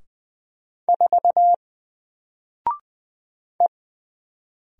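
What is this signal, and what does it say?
Morse code, heard as a keyed tone, about 20 wpm, 700 Hz. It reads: 4 E
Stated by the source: Single letter and number Morse drill at 20 wpm, from N4LCD.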